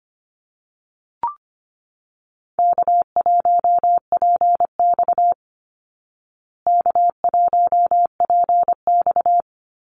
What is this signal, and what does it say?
Morse code, heard as a keyed tone, about 25 wpm, 700 Hz. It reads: K1PX K1PX
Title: Easy Call Signs Repeated 3x - 25wpm